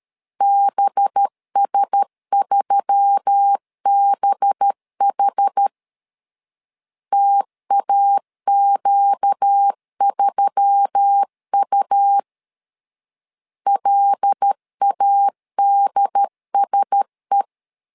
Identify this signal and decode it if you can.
Morse code, heard as a keyed tone, about 13 wpm, 800 Hz. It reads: BS3BH TAQ3U LADSE
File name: LEZIONI / 5 LBVC / 5 LB 12wpm 1906